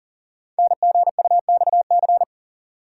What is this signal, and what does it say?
Morse code, heard as a keyed tone, about 42 wpm, 700 Hz. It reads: NGUXC
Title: Code Groups - 5 Letter:Numbers - 40wpm